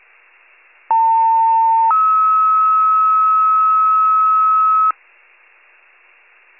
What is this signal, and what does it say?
Two-tone page, heard as a one-second tone, then a three-second tone, about 900 then 1290 Hz. It is heard from 0.9 to 4.9 s.